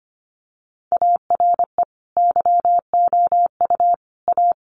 Morse code, heard as a keyed tone, about 25 wpm, 700 Hz. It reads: ARE YOU A
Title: Sentences from Top 1500 Words 25wpm